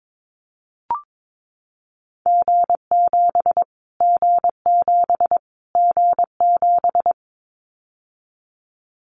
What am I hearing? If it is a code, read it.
Morse code, heard as a keyed tone, about 22 wpm, 700 Hz. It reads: G7 G7 G7
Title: Callsign Prefixes for Top 25 DXCC - Repeated 3x - 22wpm